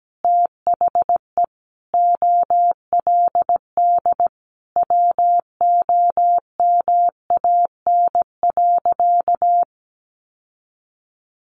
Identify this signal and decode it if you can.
Morse code, heard as a keyed tone, about 17 wpm, 700 Hz. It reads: THE OLD WOMAN.
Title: Sentences from Top 1100 Words 17wpm